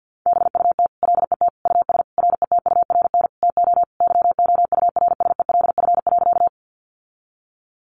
Morse code, heard as a keyed tone, about 50 wpm, 700 Hz. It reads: BUT LET US RETURN TO YOURSELF.
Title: Sentences from Top 800 Words 50wpm